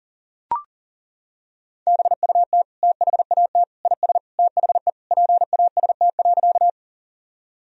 Morse code, heard as a keyed tone, about 37 wpm, 700 Hz. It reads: BUT THAT IS THE PAST.
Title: Sentences from Top 1000 Words - Review Part 3 - 40wpm